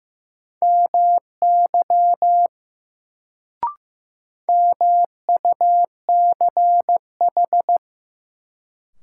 Morse code, heard as a keyed tone, about 15 wpm, 700 Hz. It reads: MY MUCH
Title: Sentences from Top 200 Words - New Word Drill - 15wpm